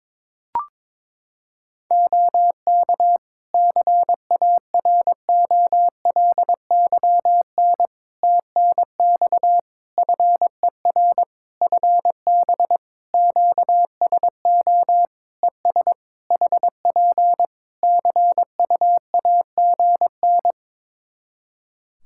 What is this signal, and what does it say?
Morse code, heard as a keyed tone, about 22 wpm, 700 Hz. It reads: OK CAROLYN TNX FER FB QSO ES HP CUAGN
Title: Practice QSO - Exchange 4 - Ending - 22wpm